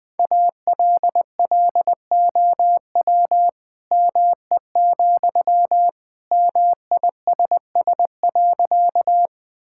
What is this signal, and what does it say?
Morse code, heard as a keyed tone, about 20 wpm, 700 Hz. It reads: ALLOW ME, MISS.
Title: Sentences from Top 1000 Words - Review Part 5 - 20wpm